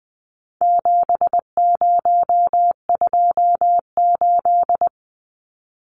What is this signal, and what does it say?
Morse code, heard as a keyed tone, about 20 wpm, 700 Hz. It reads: 7028
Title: Numbers - 4-Digit 20wpm